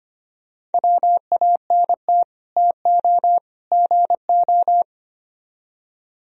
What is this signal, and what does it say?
Morse code, heard as a keyed tone, about 33 wpm, 700 Hz. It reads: WANT TO GO